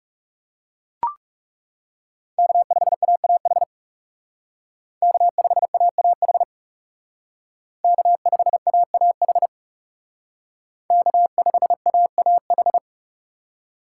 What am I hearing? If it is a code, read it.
Morse code, heard as a keyed tone, about 36 wpm, 700 Hz. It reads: K5AAH K5AAH K5AAH K5AAH
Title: Speed Racing - USA 1x3 Call Signs 30wpm